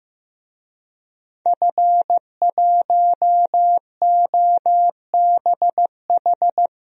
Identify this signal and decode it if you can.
Morse code, heard as a keyed tone, about 15 wpm, 700 Hz. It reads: F1OBH
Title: DXCC-100 Call Signs - 5 Characters - 15wpm